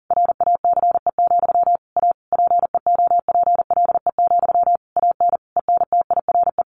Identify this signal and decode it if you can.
Morse code, heard as a keyed tone, about 40 wpm, 700 Hz. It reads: RACE, A PEOPLE, AN ENTIRE
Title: A Sound Of Thunder 40wpm